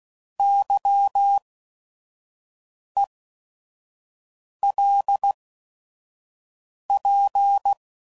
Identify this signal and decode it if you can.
Morse code, heard as a keyed tone, about 16 wpm, spaced as for 5 wpm, 800 Hz. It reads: YELP